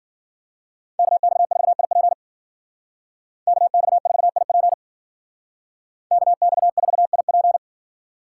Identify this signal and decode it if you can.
Morse code, heard as a keyed tone, about 52 wpm, 700 Hz. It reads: KX4IP KX4IP KX4IP